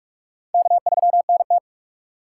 Morse code, heard as a keyed tone, about 46 wpm, 700 Hz. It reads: K3NT
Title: USA Call Signs - 4 Characters - 3x Character Spacing - 45wpm